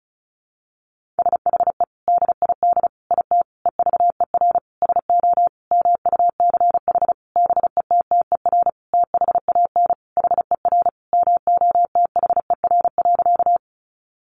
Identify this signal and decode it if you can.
Morse code, heard as a keyed tone, about 34 wpm, 700 Hz. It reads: SHE DID IT EVER SO MUCH BETTER THAN HER MOTHER.
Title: Sentences from Top 400 Words 35wpm